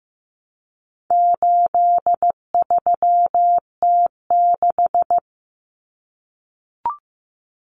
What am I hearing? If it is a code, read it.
Morse code, heard as a keyed tone, about 15 wpm, 700 Hz. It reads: 83T6
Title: Common Cut Numbers - 4-Digit 15wpm